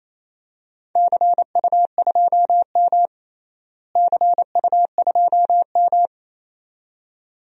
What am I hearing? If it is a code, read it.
Morse code, heard as a keyed tone, about 28 wpm, 700 Hz. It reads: CU2M CU2M
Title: DXCC-100 Call Signs - 4 Characters - Repeated 2x - 28wpm